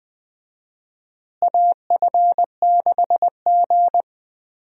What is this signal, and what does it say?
Morse code, heard as a keyed tone, about 20 wpm, 700 Hz. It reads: AF6G
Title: USA Call Signs - 4 Characters - Repeated 2x - 20wpm